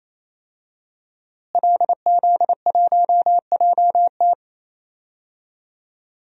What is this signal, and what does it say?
Morse code, heard as a keyed tone, about 28 wpm, 700 Hz. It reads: LZ1JT
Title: DXCC-100 Call Signs - 5 Characters - Repeated 3x - 28wpm